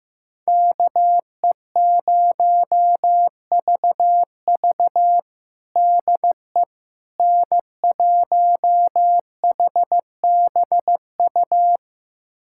Morse code, heard as a keyed tone, about 16 wpm, 700 Hz. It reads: KE0VV DE N1HBU
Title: USA Callsign DE Callsign - 5 Characters - 15wpm